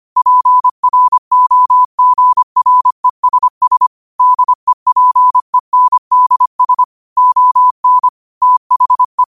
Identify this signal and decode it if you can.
Morse code, heard as a keyed tone, about 25 wpm, 1000 Hz. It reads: PROGRESS DEPENDS ON THE
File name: qotdfiles / 25wpm